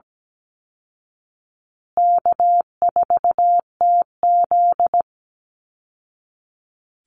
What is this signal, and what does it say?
Morse code, heard as a keyed tone, about 17 wpm, 700 Hz. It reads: K4TZ
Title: Easy Call Signs Repeated 2x - 17wpm